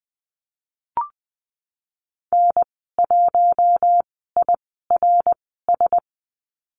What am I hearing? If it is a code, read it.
Morse code, heard as a keyed tone, about 20 wpm, spaced as for 15 wpm, 700 Hz. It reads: N1IRS
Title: 055 Lesson 10g - Letter r - Call Signs 20wpm